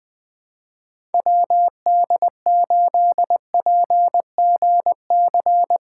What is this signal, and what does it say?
Morse code, heard as a keyed tone, about 21 wpm, 700 Hz. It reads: WD8PGC
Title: USA Callsign DE Callsign - 6 Characters - 20wpm